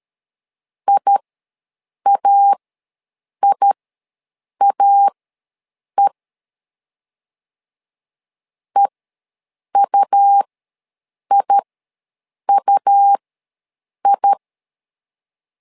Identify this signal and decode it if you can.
Morse code, heard as a keyed tone, about 13 wpm, spaced as for 7 wpm, 800 Hz. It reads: IAIAE EUIUI
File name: LEZIONI / 2 UHDN / 2 U ws 12wpm 1655